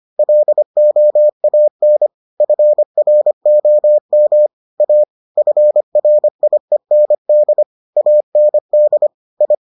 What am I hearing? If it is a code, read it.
Morse code, heard as a keyed tone, about 25 wpm, 600 Hz. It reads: LOAN FROM A FRIEND AND I